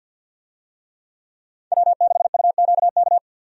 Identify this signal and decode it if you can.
Morse code, heard as a keyed tone, about 49 wpm, 700 Hz. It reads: W6UQK
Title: USA Call Signs - 5 Characters - 50wpm